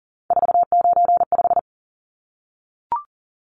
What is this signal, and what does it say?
Morse code, heard as a keyed tone, about 40 wpm, 700 Hz. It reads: 495
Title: Numbers - 3-Digit 40wpm